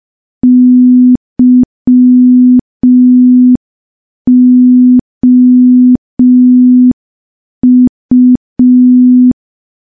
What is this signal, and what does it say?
Morse code, heard as a keyed tone, about 5 wpm, 250 Hz. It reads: YOU
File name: morse / morse-training-05WPM